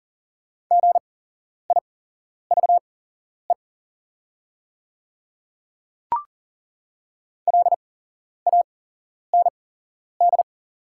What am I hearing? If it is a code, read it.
Morse code, heard as a keyed tone, about 40 wpm, spaced as for 11 wpm, 700 Hz. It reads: GIVE LAND
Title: 8x Character Spacing - Top 200 Words - 40wpm